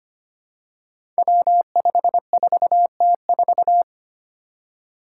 Morse code, heard as a keyed tone, about 25 wpm, 700 Hz. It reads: W54T4
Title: Five Challenging Letter-Number Combinations 25wpm